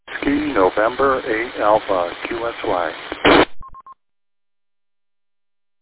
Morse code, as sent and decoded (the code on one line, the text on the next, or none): none